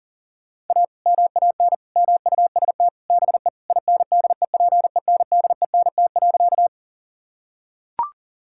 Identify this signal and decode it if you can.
Morse code, heard as a keyed tone, about 37 wpm, 700 Hz. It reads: A MAN MUST BE INDEPENDENT.
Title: Sentences from Top 1000 Words - Review Part 6 - 40wpm